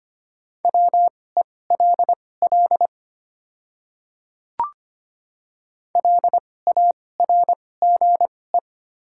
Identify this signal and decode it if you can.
Morse code, heard as a keyed tone, about 25 wpm, spaced as for 18 wpm, 700 Hz. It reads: WELL LARGE
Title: 2x Character Spacing - Top 200 Words - 25wpm